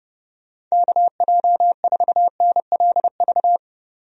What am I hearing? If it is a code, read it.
Morse code, heard as a keyed tone, about 30 wpm, 700 Hz. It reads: KJ4NLV
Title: USA Call Signs - 6 Characters - Repeated 2x - 30wpm